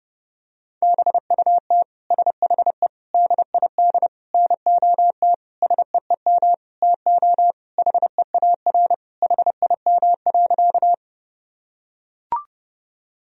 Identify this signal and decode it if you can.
Morse code, heard as a keyed tone, about 32 wpm, 700 Hz. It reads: BUT SHE DID NOT SEEM TO HEAR HIM.
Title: Sentences from Top 1000 Words - Review Part 1 - 30wpm